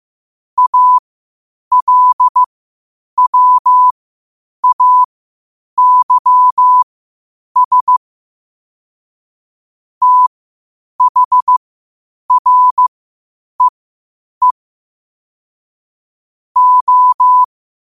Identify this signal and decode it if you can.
Morse code, heard as a keyed tone, about 15 wpm, spaced as for 8 wpm, 1000 Hz. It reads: ALWAYS THREE O
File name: qotdfiles / 8wpm